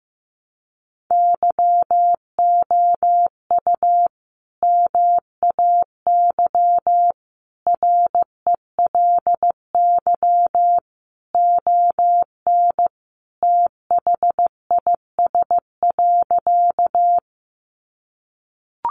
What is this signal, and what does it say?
Morse code, heard as a keyed tone, about 15 wpm, 700 Hz. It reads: YOU MAY RELY ON THIS.
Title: Sentences from Top 1200 Words 15wpm